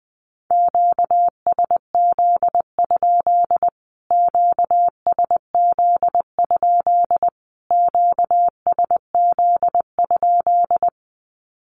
Morse code, heard as a keyed tone, about 20 wpm, 700 Hz. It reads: QSZ? QSZ? QSZ?
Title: Q-Codes Repeated 3x 20wpm